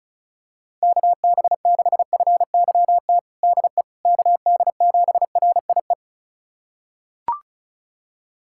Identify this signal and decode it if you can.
Morse code, heard as a keyed tone, about 34 wpm, 690 Hz. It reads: KB6FYT DE KD7RIE
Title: USA Callsign DE Callsign - 6 Characters - 35wpm